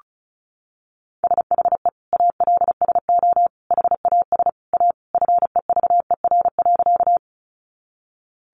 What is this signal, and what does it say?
Morse code, heard as a keyed tone, about 34 wpm, 700 Hz. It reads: SHE ALSO HAS A FEVER.
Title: Easy Phrases 35wpm